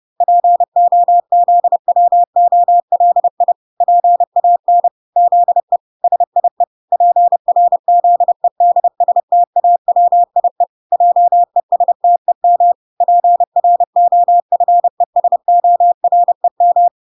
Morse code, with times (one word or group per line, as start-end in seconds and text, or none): 0.2-3.5 POZWOLI
3.8-4.9 PAN
5.2-5.8 ZE
6.0-6.6 SIE
6.9-10.6 PRZEDSTAWIE
10.9-12.7 JESTEM
13.0-16.9 PROFESOREM